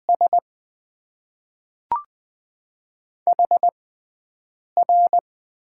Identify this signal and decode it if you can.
Morse code, heard as a keyed tone, about 20 wpm, spaced as for 7 wpm, 700 Hz. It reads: S HR
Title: Top 90 QSO Elements - 6x Character Spacing - 20wpm